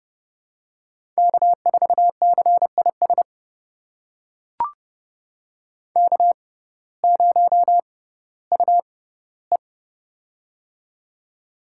Morse code, heard as a keyed tone, about 32 wpm, spaced as for 17 wpm, 700 Hz. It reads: K4CIS K0UE
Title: USA Call Signs - 6x Character Spacing - 30wpm